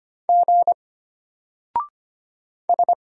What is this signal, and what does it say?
Morse code, heard as a keyed tone, about 25 wpm, 700 Hz. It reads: G S
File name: Single-Letter-Number-25wpm